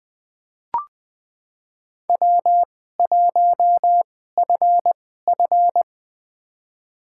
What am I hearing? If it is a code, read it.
Morse code, heard as a keyed tone, about 20 wpm, spaced as for 15 wpm, 700 Hz. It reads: W1FF
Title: USA Call Signs - 4 Characters - 2x Character Spacing - 20wpm